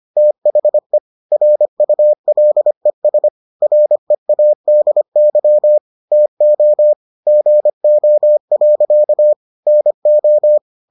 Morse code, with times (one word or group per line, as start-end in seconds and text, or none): 0.2-1.0 THE
1.3-3.3 RULES
3.6-5.8 READY
6.1-6.9 TO
7.3-9.3 GO.
9.7-10.6 NO